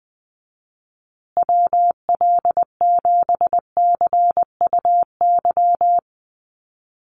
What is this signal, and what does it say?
Morse code, heard as a keyed tone, about 20 wpm, 700 Hz. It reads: WL7CUY
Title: USA Call Signs Repeated 2x 20wpm